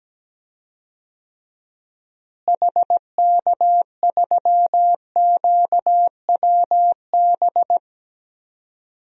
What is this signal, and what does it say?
Morse code, heard as a keyed tone, about 17 wpm, 700 Hz. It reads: HK3QWB
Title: DXCC-100 Call Signs - 6 Characters - Repeated 2x - 17wpm